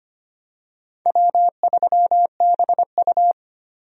Morse code, heard as a keyed tone, about 25 wpm, 700 Hz. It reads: W3BU
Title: USA Call Signs - 4 Characters - 25wpm